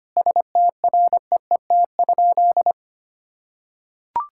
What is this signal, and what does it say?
Morse code, heard as a keyed tone, about 25 wpm, 700 Hz. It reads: STREET?
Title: Sentences from Top 1600 Words 25wpm